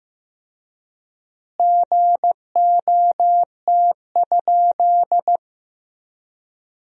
Morse code, heard as a keyed tone, about 15 wpm, 700 Hz. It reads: GOT?
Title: 4x Word-Spacing - Sentences from Top 300 Words - 15wpm